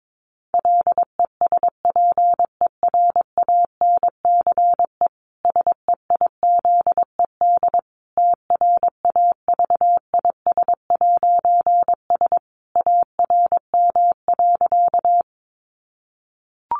Morse code, heard as a keyed tone, about 22 wpm, 700 Hz. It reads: LESPERANCE SEIZED TRAVIS'S ARM.
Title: A Sound Of Thunder 22wpm